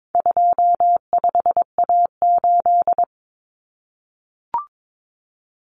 Morse code, heard as a keyed tone, about 22 wpm, 700 Hz. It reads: 25A8
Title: Common Cut Numbers - 4-Digit 22wpm